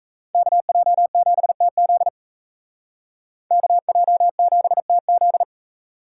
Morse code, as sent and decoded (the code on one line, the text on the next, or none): -.- .--- --... - --.. / -.- .--- --... - --..
KJ7TZ KJ7TZ